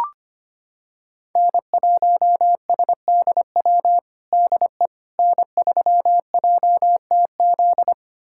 Morse code, heard as a keyed tone, about 25 wpm, 700 Hz. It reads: N1SDW DE N3JTZ
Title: USA Callsign DE Callsign - 5 Characters - 25wpm